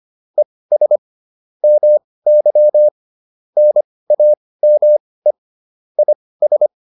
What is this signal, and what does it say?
Morse code, heard as a keyed tone, about 25 wpm, spaced as for 18 wpm, 600 Hz. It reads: ES MY NAME IS
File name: QSO306 18